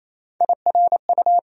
Morse code, heard as a keyed tone, about 28 wpm, 700 Hz. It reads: IRU
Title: USA Call Sign Postfixes - 3 Letters 28wpm